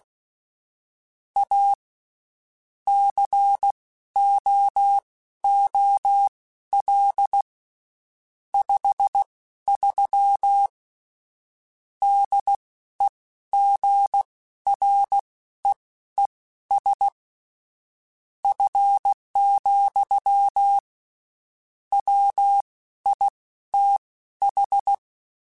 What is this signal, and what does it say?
Morse code, heard as a keyed tone, about 16 wpm, spaced as for 12 wpm, 800 Hz. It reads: A COOL 53 DEGREES F, WITH
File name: Spaced Characters / Spaced Characters-mp3s / Weather 5 1 sp @15